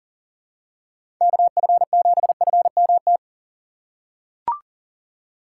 Morse code, heard as a keyed tone, about 40 wpm, 700 Hz. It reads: KF7FMT
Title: USA Call Signs - 6 Characters - 4x Character Spacing - 40wpm